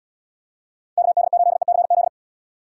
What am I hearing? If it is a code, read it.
Morse code, heard as a keyed tone, about 76 wpm, 700 Hz. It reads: KD8ECZ